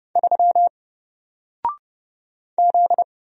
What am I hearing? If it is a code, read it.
Morse code, heard as a keyed tone, about 30 wpm, 700 Hz. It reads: 3 Z
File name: Single-Letter-Number-30wpm